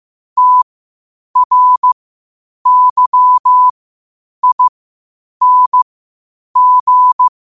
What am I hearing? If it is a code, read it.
Morse code, heard as a keyed tone, about 15 wpm, spaced as for 8 wpm, 1000 Hz. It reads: TRYING
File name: qotdfiles / 8wpm